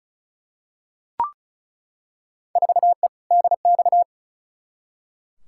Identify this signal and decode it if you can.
Morse code, heard as a keyed tone, about 34 wpm, 700 Hz. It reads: 4E DX
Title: Field Day Exchanges Encore - 35wpm